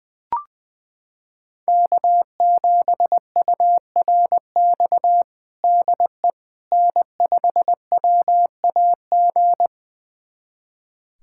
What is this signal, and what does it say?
Morse code, heard as a keyed tone, about 20 wpm, 700 Hz. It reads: K7URX DE N5WAG
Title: USA Callsign DE Callsign - 5 Characters - 20wpm